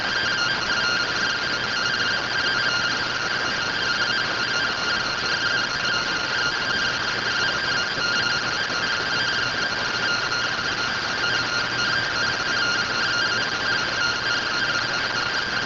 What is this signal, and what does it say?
Two-tone teleprinter signal, about 170 Hz shift.